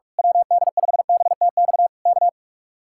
Morse code, heard as a keyed tone, about 46 wpm, 700 Hz. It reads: WD5BTX K
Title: Speed-Racing Common QSO Phrases - 45wpm